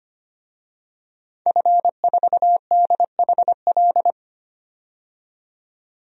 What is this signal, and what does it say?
Morse code, heard as a keyed tone, about 25 wpm, 700 Hz. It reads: F4DHL
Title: DXCC-100 Call Signs - 5 Characters - 25wpm